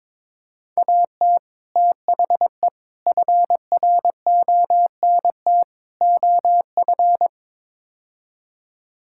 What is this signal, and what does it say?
Morse code, heard as a keyed tone, about 21 wpm, 700 Hz. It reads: AT THE FRONT OF
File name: practice / 22wpm / Build A Phrase - 5-Word Phrases - 22wpm